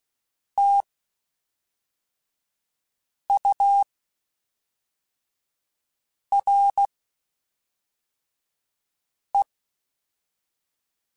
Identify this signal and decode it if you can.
Morse code, heard as a keyed tone, about 16 wpm, 800 Hz. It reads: T U R E
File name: Spaced Characters / Spaced Characters-mp3s / Weather5 22 sp @5